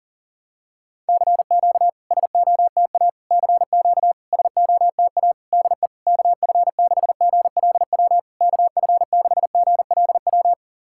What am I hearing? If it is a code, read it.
Morse code, heard as a keyed tone, about 40 wpm, 690 Hz. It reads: CQ SOTA CQ SOTA DE KF6GLW KF6GLW
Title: SOTA QSO Practice 40wpm